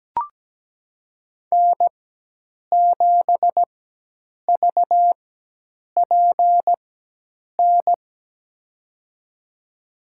Morse code, heard as a keyed tone, about 17 wpm, spaced as for 8 wpm, 700 Hz. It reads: N7VPN
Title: USA Call Signs - 5 Characters - 4x Character Spacing - 17wpm